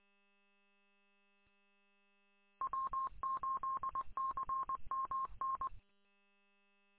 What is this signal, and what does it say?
Morse code, heard as a keyed tone, about 23 wpm, 1070 Hz. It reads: W8CMN
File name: w8cmn- -combined